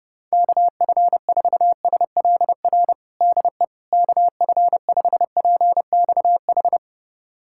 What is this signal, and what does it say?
Morse code, heard as a keyed tone, about 30 wpm, 700 Hz. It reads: KF4SLR DE KF5PXH
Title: USA Callsign DE Callsign - 6 Characters - 30wpm